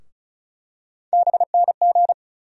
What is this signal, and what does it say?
Morse code, heard as a keyed tone, about 36 wpm, 700 Hz. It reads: BNG
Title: USA Call Sign Postfixes - 3 Letters 35wpm